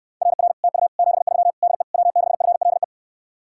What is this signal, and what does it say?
Morse code, heard as a keyed tone, about 69 wpm, 700 Hz. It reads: RR TU 73 DE W6FLE